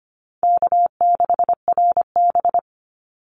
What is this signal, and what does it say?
Morse code, heard as a keyed tone, about 25 wpm, 700 Hz. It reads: K6RB
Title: Easy Call Signs Repeated 3x - 25wpm